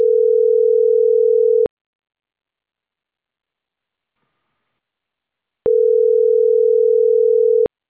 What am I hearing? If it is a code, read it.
Ringback tone (North America), 440 + 480 Hz.